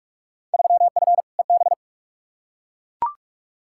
Morse code, heard as a keyed tone, about 42 wpm, 700 Hz. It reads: 3F EB
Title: Field Day Exchanges - 45wpm